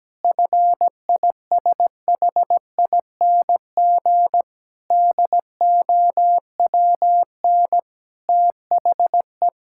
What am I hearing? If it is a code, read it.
Morse code, heard as a keyed tone, about 17 wpm, 700 Hz. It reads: FISHING DOWN THE
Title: Sentences from Top 1000 Words - Review Part 5 - 17wpm